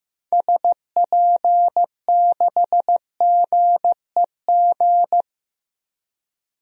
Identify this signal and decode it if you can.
Morse code, heard as a keyed tone, about 15 wpm, 700 Hz. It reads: SP6GEG
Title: DXCC-100 Call Signs - 6 Characters - Repeated 3x - 15wpm